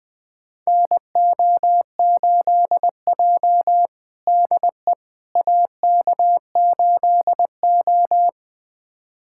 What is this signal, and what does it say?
Morse code, heard as a keyed tone, about 20 wpm, 700 Hz. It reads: NO8J DE AK8O
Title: USA Callsign DE Callsign - 4 Characters - 20wpm